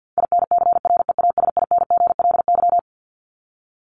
Morse code, heard as a keyed tone, about 50 wpm, 700 Hz. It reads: INCREASINGLY
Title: LY Suffix - Top 100 Words with Base Words 50wpm